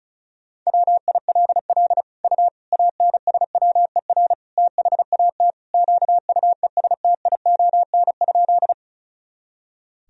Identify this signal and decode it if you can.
Morse code, heard as a keyed tone, about 36 wpm, 700 Hz. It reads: WILL U ANSWER THAT QUESTION?
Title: Sentences from Top 400 Words and Abbreviations 35wpm